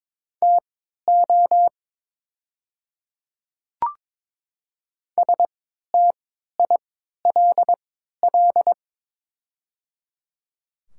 Morse code, heard as a keyed tone, about 22 wpm, spaced as for 12 wpm, 700 Hz. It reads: TO STILL
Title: 3x Character Spacing - Top 500 Words - 22wpm